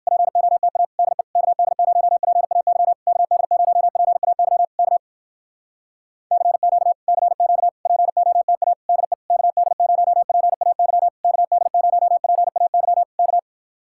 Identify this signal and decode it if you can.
Morse code, heard as a keyed tone, about 59 wpm, 700 Hz. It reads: POTA DE KD0PAY KD0PAY K CQ CQ POTA DE KD0PAY KD0PAY K